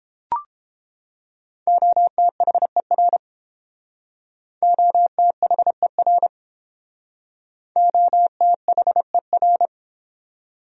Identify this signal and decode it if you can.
Morse code, heard as a keyed tone, about 31 wpm, 700 Hz. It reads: OTHER OTHER OTHER